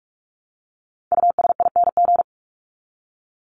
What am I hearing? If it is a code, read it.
Morse code, heard as a keyed tone, about 44 wpm, 700 Hz. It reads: USING
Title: ING Suffix - Top 100 Words with Base Words 45wpm